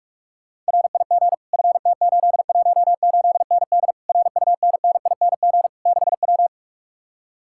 Morse code, heard as a keyed tone, about 66 wpm, spaced as for 55 wpm, 700 Hz. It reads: RIG FT818ND RUNNING 6W
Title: Speed-Racing Common QSO Phrases Encore - 45wpm